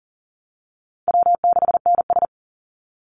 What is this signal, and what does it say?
Morse code, heard as a keyed tone, about 40 wpm, 700 Hz. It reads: W6NS
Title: Easy Call Signs Repeated 2x - 40wpm